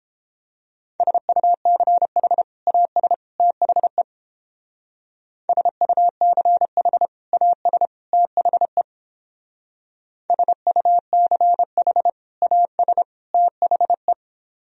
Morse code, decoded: SUCH AS THE SUCH AS THE SUCH AS THE

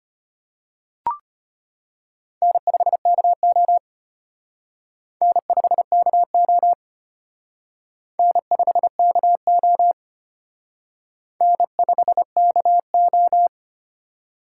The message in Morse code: N5KO N5KO N5KO N5KO

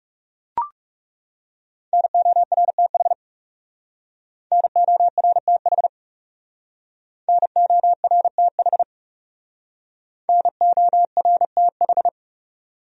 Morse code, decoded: NORTH NORTH NORTH NORTH

